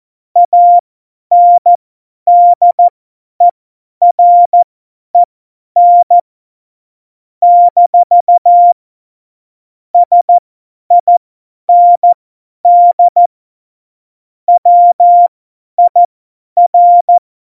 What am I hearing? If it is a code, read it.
Morse code, heard as a keyed tone, about 14 wpm, spaced as for 10 wpm, 700 Hz. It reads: ANDEREN - SIND WIR
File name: Morsekurs / Witze / Witz 17 April 2018, 50 F70